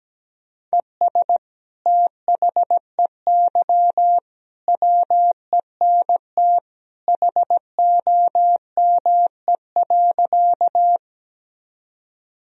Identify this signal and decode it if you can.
Morse code, heard as a keyed tone, about 17 wpm, 700 Hz. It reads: ES THEY WENT HOME.
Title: Sentences from Top 200 Words and Abbreviations 17wpm